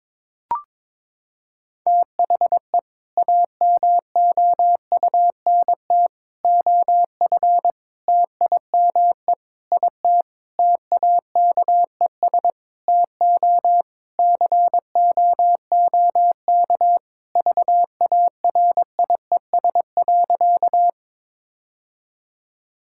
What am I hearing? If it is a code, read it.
Morse code, heard as a keyed tone, about 22 wpm, 700 Hz. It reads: THE AMOUNT OF TIME IT TAKES TO COOK VARIES.
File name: Sentences with 7-Word Phrases - 22wpm